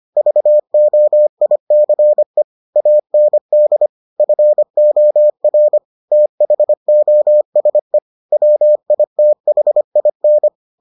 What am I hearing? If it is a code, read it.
Morse code, heard as a keyed tone, about 25 wpm, 600 Hz. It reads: VOICE AND FOR THOSE WITHIN